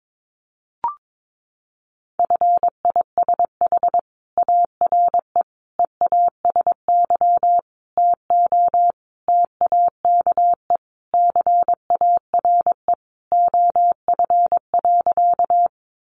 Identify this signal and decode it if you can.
Morse code, heard as a keyed tone, about 22 wpm, 700 Hz. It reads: FISH ARE EASY TO TAKE CARE OF.
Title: Easy Phrases 22wpm